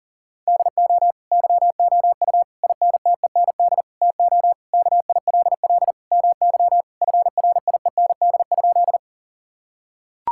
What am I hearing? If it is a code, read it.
Morse code, heard as a keyed tone, about 40 wpm, 700 Hz. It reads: DO YOU INTEND TO KILL MY FRIEND?
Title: Sentences from Top 1600 Words 40wpm